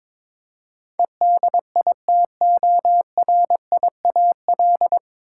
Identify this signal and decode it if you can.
Morse code, heard as a keyed tone, about 22 wpm, 700 Hz. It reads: EDITORIAL
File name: practice / 22wpm / Nine Letter Words - 22wpm